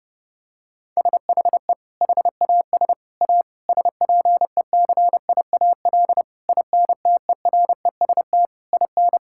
Morse code, heard as a keyed tone, about 30 wpm, 700 Hz. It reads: SHE HAS A SPECIAL INTEREST IN
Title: Sentences from Top 1000 Words - Review Part 5 - 30wpm